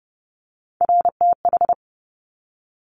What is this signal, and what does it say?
Morse code, heard as a keyed tone, about 30 wpm, 700 Hz. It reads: RTH
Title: Top Three Letter Combinations in Words 30wpm